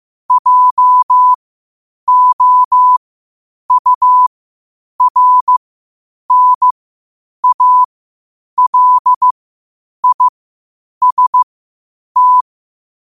Morse code, heard as a keyed tone, about 15 wpm, spaced as for 8 wpm, 1000 Hz. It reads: JOURNALIST
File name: qotdfiles / 8wpm